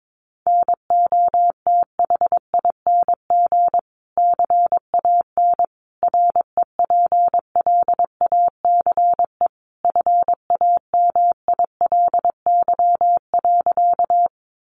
Morse code, heard as a keyed tone, about 22 wpm, 700 Hz. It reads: NOTHING CAN REPLACE FAMILY.
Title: Sentences from Top 1400 Words 22wpm